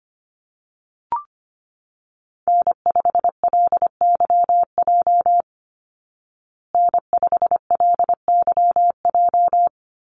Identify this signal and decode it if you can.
Morse code, heard as a keyed tone, about 25 wpm, 700 Hz. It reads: N5LYJ N5LYJ
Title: Call Signs Repeated 2x - 25wpm